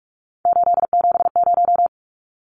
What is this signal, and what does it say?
Morse code, heard as a keyed tone, about 44 wpm, 700 Hz. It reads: 870
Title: Numbers - 3-Digit 45wpm